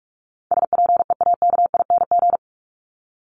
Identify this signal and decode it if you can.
Morse code, heard as a keyed tone, about 44 wpm, 700 Hz. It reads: SPEAKING
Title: ING Suffix - Top 100 Words with Base Words 45wpm